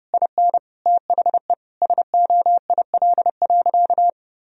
Morse code, heard as a keyed tone, about 30 wpm, 700 Hz. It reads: IN THE SOIL.
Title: Sentences from Top 1000 Words - Review Part 4 - 30wpm